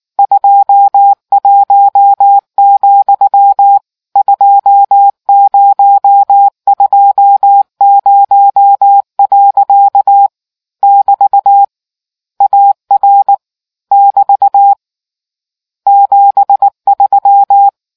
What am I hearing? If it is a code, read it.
Morse code, heard as a keyed tone, about 19 wpm, 800 Hz. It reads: 21, 2020. <BT> AR <BT> 73